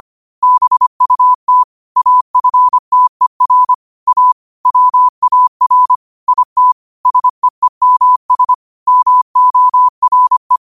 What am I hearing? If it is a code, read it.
Morse code, heard as a keyed tone, about 25 wpm, 1000 Hz. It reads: BUT AFTER A WAR IT SEEMS MORE